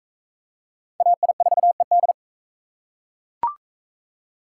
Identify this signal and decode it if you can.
Morse code, heard as a keyed tone, about 39 wpm, 700 Hz. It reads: AI4ED